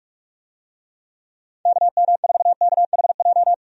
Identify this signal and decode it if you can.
Morse code, heard as a keyed tone, about 47 wpm, 700 Hz. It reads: KM4KHJ
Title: USA Call Signs - 6 Characters - 45wpm